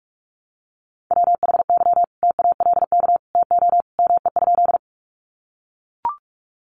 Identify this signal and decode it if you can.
Morse code, heard as a keyed tone, about 44 wpm, 700 Hz. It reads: WHY TALK TO ME?
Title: Sentences from Top 400 Words 45wpm